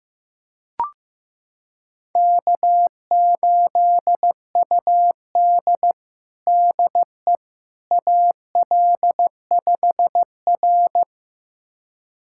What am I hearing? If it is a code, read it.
Morse code, heard as a keyed tone, about 15 wpm, 700 Hz. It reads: K8UD DE AL5R
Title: USA Callsign DE Callsign - 4 Characters - 15wpm